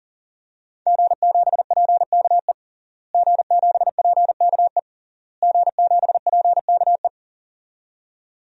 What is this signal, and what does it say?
Morse code, heard as a keyed tone, about 40 wpm, 700 Hz. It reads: G7PKE G7PKE G7PKE
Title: DXCC-100 Call Signs - 5 Characters - Repeated 3x - 40wpm